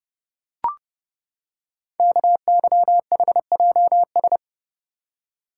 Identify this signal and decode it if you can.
Morse code, heard as a keyed tone, about 30 wpm, 700 Hz. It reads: KYHJS
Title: Code Groups - 5 Letters - 30wpm